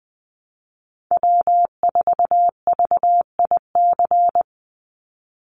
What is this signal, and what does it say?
Morse code, heard as a keyed tone, about 19 wpm, 700 Hz. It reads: W4VIC
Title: Call Signs Repeated 3x - 20wpm